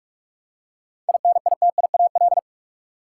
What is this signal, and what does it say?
Morse code, heard as a keyed tone, about 46 wpm, 700 Hz. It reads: INITIAL
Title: Seven Letter Words - 45wpm